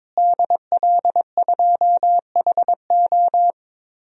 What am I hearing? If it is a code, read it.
Morse code, heard as a keyed tone, about 22 wpm, 700 Hz. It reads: DL2HO
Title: DXCC-100 Call Signs - 5 Characters - Repeated 2x - 22wpm